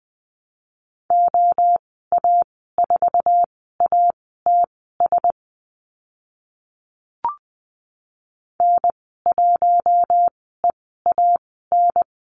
Morse code, heard as a keyed tone, about 20 wpm, spaced as for 14 wpm, 700 Hz. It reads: OA4ATS N1EAN
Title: 055 Lesson 10g - Letter r - Call Signs 20wpm